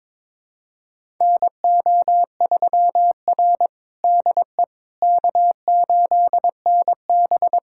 Morse code, heard as a keyed tone, about 22 wpm, 700 Hz. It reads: NO3R DE K8NB
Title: USA Callsign DE Callsign - 4 Characters - 22wpm